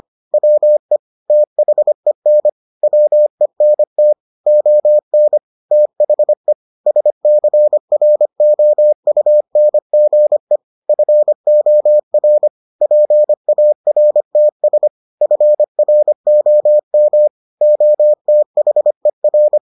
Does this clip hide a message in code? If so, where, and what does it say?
Yes: Morse code from 0.3 to 19.6 s, heard as a keyed tone, about 25 wpm, 600 Hz. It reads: WE THEN WENT ON THE SCROUNGE FOR PARTS FROM OTHER